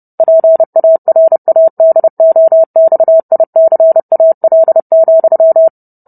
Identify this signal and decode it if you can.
Morse code, heard as a keyed tone, about 30 wpm, 650 Hz. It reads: PARADOXICAL,